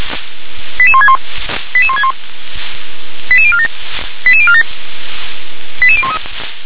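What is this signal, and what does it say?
Five-tone selective call (ZVEI), heard as quick runs of five tones, tones of about 70 ms. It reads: 88161 88161 88047 88047 88013
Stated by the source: ZVEI five-tone selective call received on an SDR.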